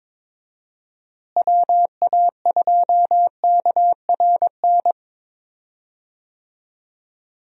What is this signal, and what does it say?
Morse code, heard as a keyed tone, about 22 wpm, 700 Hz. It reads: WA2KRN
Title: USA Call Signs - 6 Characters - Repeated 2x - 22wpm